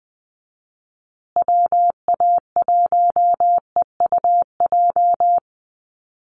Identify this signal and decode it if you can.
Morse code, heard as a keyed tone, about 20 wpm, 700 Hz. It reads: WA1EUJ
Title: USA Call Signs Repeated 3x 20wpm